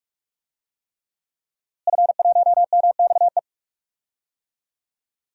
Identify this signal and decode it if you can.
Morse code, heard as a keyed tone, about 46 wpm, 700 Hz. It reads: F1MXE
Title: DXCC-100 Call Signs - 5 Characters - Repeated 3x - 45wpm